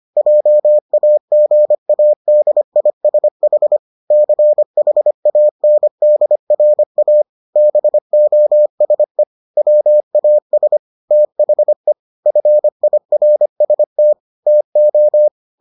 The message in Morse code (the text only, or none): JAGADISH CHANDRA BOSE WAS THE FIRST TO